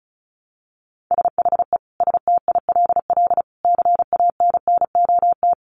Morse code, heard as a keyed tone, about 34 wpm, 700 Hz. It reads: SHE STILL CANNOT
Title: Additional Easy Phrases 35wpm